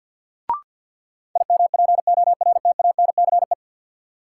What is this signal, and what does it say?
Morse code, heard as a keyed tone, about 47 wpm, 700 Hz. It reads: IMPORTANCE
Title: Sentences from Top 1200 Words - New Word Drill - 50wpm